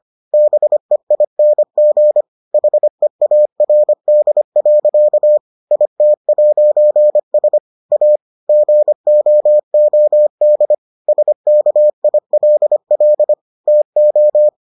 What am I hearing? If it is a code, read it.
Morse code, heard as a keyed tone, about 25 wpm, 600 Hz. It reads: BEING HEARD. IT'S A GOOD SKILL TO